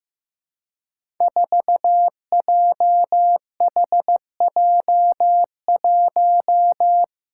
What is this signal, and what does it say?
Morse code, heard as a keyed tone, about 15 wpm, 700 Hz. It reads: 4JHJ1
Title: Five Challenging Letter-Number Combinations 15wpm